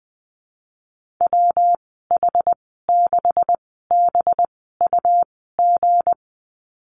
Morse code, heard as a keyed tone, about 20 wpm, spaced as for 15 wpm, 700 Hz. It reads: WH6BUG
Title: 193 Lesson 31g - Letter v - Call Signs 20wpm